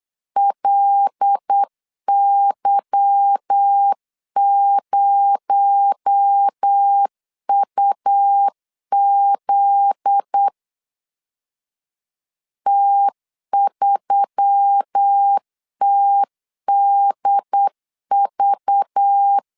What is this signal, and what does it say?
Morse code, heard as a keyed tone, about 8 wpm, 800 Hz. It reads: LY0UZ T3TDV
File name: LEZIONI / 8 YJ45 / 8 Y 8wpm 3245